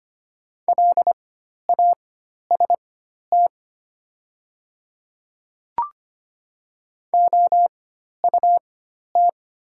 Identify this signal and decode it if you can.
Morse code, heard as a keyed tone, about 25 wpm, spaced as for 12 wpm, 700 Hz. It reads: LAST OUT
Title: 4x Character Spacing - Top 500 Words - 25wpm